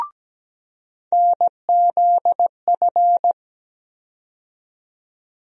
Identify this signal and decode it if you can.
Morse code, heard as a keyed tone, about 17 wpm, 700 Hz. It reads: NZF
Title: USA Call Sign Postfixes - 3 Letters 17wpm